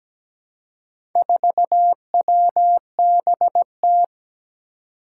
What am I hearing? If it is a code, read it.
Morse code, heard as a keyed tone, about 17 wpm, 700 Hz. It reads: 4WBT
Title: Four Challenging Letter-Number Combinations 17wpm